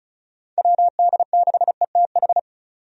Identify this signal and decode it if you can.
Morse code, heard as a keyed tone, about 34 wpm, 700 Hz. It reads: WD6ETH
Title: USA Call Signs - 6 Characters - 35wpm